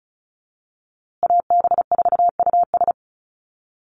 Morse code, heard as a keyed tone, about 34 wpm, 700 Hz. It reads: AB4US
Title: USA Call Signs Repeated 2x 35wpm